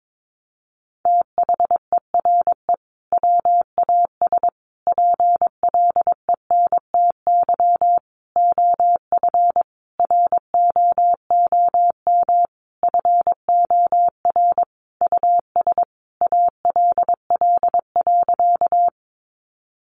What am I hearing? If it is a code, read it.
Morse code, heard as a keyed tone, about 22 wpm, 700 Hz. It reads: THERE WAS PLENTY OF ROOM FOR US ALL.
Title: Sentences from Top 1300 Words 22wpm